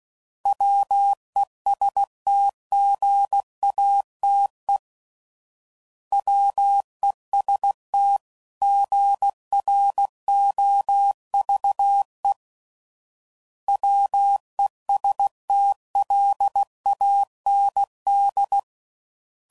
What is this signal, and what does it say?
Morse code, heard as a keyed tone, about 16 wpm, 800 Hz. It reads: WESTGATE WEST GROVE WESTLAND
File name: CW Code Groups @15WPM / fromSM's / Cities